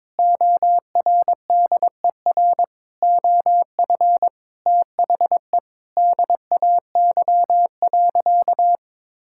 Morse code, heard as a keyed tone, about 22 wpm, 700 Hz. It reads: ORDER OF THE DAY.
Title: Sentences from Top 1000 Words - Review Part 5 - 22wpm